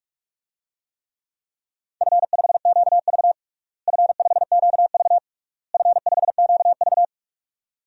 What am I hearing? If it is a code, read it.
Morse code, heard as a keyed tone, about 46 wpm, 700 Hz. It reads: F5QV F5QV F5QV
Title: DXCC-100 Call Signs - 4 Characters - Repeated 3x - 45wpm